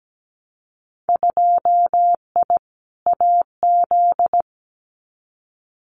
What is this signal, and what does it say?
Morse code, heard as a keyed tone, about 17 wpm, 700 Hz. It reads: 2I AZ
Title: Winter Field Day Exchanges 17wpm